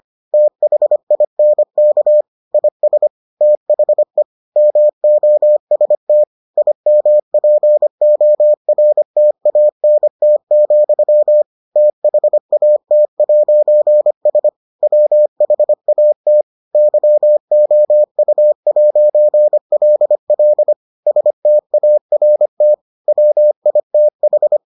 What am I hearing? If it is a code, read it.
Morse code, heard as a keyed tone, about 25 wpm, 600 Hz. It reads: THINK IS THE MOST IMPORTANT, THAT'S WHAT YOU'LL START WITH